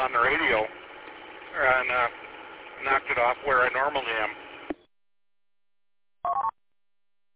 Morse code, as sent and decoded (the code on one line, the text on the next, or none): none